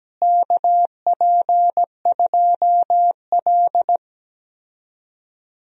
Morse code, heard as a keyed tone, about 17 wpm, 700 Hz. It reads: KP2L
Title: DXCC-100 Call Signs - 4 Characters - Repeated 2x - 17wpm